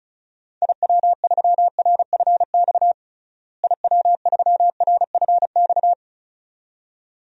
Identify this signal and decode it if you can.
Morse code, heard as a keyed tone, about 36 wpm, 700 Hz. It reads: IW3RFX IW3RFX
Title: DXCC-100 Call Signs - 6 Characters - Repeated 2x - 35wpm